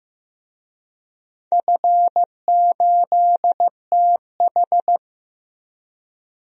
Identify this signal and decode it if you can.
Morse code, heard as a keyed tone, about 16 wpm, 700 Hz. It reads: F8TH
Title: DXCC-100 Call Signs - 4 Characters - Repeated 2x - 15wpm